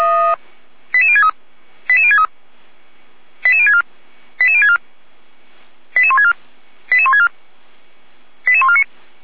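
Five-tone selective call (ZVEI), heard as quick runs of five tones, tones of about 70 ms. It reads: 88063 88063 88064 88064 88164 88164 88149
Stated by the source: Five-tone ZVEI selective call received by SDR.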